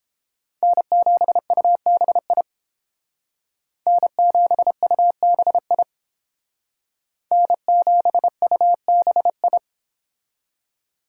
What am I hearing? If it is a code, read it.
Morse code, heard as a keyed tone, about 28 wpm, 700 Hz. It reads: N7UBI N7UBI N7UBI